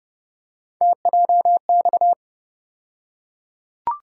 Morse code, heard as a keyed tone, about 30 wpm, 700 Hz. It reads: TJX
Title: USA Call Sign Postfixes - 3 Letters 30wpm